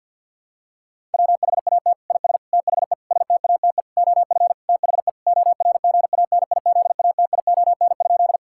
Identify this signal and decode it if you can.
Morse code, heard as a keyed tone, about 50 wpm, 700 Hz. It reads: WHAT IS THE STATE OF THE ORGANIZATION?
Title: Sentences from Top 1000 Words - Review Part 5 - 50wpm